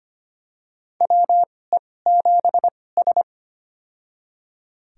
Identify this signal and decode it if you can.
Morse code, heard as a keyed tone, about 25 wpm, spaced as for 18 wpm, 700 Hz. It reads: WE7S
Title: USA Call Signs - 4 Characters - 2x Character Spacing - 25wpm